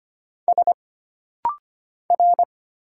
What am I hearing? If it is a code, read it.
Morse code, heard as a keyed tone, about 25 wpm, 700 Hz. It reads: S R
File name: Single Character (Letter:Number:Punctuation:Prosign) - Rapid-Fire - 25wpm